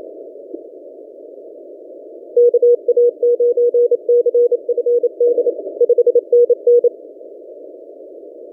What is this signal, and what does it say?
Morse code, heard as a keyed tone, about 28 wpm, 480 Hz. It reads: KA9CFD 5NN
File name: posts / v63mj-17cw- -0006z